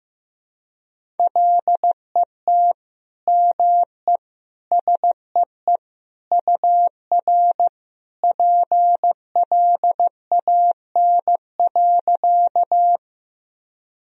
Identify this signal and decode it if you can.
Morse code, heard as a keyed tone, about 15 wpm, 700 Hz. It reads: LET ME SEE UR PLAN.
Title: Sentences from Top 500 Words and Abbreviations 15wpm